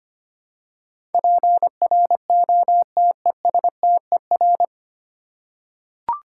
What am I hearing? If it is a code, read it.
Morse code, heard as a keyed tone, about 25 wpm, 700 Hz. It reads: PROTESTER
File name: practice / 25wpm / 9 Letter Words - First 3 Letters Given - 25wpm